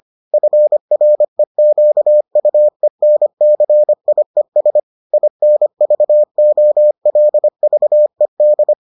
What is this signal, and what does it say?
Morse code, heard as a keyed tone, about 25 wpm, 600 Hz. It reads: FREQUENCIES INVOLVED